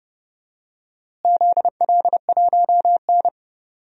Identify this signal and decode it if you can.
Morse code, heard as a keyed tone, about 31 wpm, 700 Hz. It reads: ZL1N
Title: DXCC-100 Call Signs - 4 Characters - Repeated 2x - 30wpm